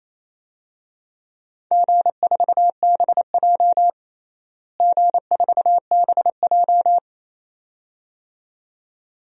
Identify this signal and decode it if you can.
Morse code, heard as a keyed tone, about 28 wpm, 700 Hz. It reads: G4BJ G4BJ
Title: DXCC-100 Call Signs - 4 Characters - Repeated 2x - 28wpm